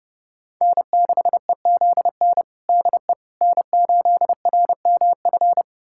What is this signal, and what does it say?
Morse code, heard as a keyed tone, about 32 wpm, 700 Hz. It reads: N6EZN DE N8RMF
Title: USA Callsign DE Callsign - 5 Characters - 30wpm